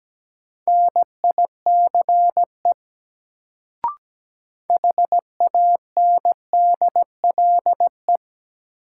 Morse code, heard as a keyed tone, about 17 wpm, 700 Hz. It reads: NICE HANDLE
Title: Sentences from Top 1100 Words - New Word Drill - 17wpm